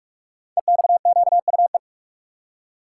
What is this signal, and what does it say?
Morse code, heard as a keyed tone, about 44 wpm, 690 Hz. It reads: EXQUE
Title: Code Groups - 5 Letter:Numbers - 45wpm